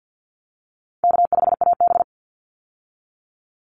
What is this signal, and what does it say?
Morse code, heard as a keyed tone, about 49 wpm, 700 Hz. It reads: K5AB
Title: Call Signs Repeated 3x - 50wpm